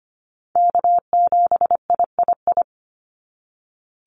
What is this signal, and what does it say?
Morse code, heard as a keyed tone, about 25 wpm, 700 Hz. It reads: K7III
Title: NA SOTA Chaser Callsigns 25wpm